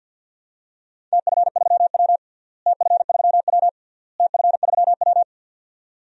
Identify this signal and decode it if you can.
Morse code, heard as a keyed tone, about 50 wpm, 700 Hz. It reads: TF3W TF3W TF3W
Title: DXCC-100 Call Signs - 4 Characters - Repeated 3x - 50wpm